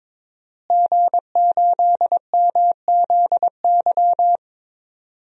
Morse code, heard as a keyed tone, about 22 wpm, 700 Hz. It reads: G8MZY
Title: DXCC-100 Call Signs - 5 Characters - Repeated 2x - 22wpm